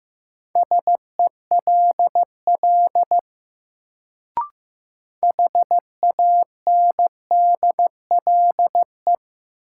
Morse code, heard as a keyed tone, about 15 wpm, 700 Hz. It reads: SELL HANDLE
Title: Sentences from Top 1100 Words - New Word Drill Alt - 15wpm